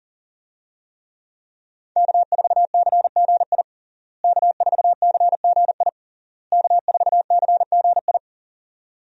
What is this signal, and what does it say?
Morse code, heard as a keyed tone, about 40 wpm, 690 Hz. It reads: K4CGI K4CGI K4CGI
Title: USA Call Signs - 5 Characters - Repeated 3x - 40wpm